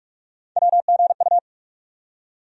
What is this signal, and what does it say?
Morse code, heard as a keyed tone, about 45 wpm, 700 Hz. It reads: WGU
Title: USA Call Sign Postfixes - 3 Letters 45wpm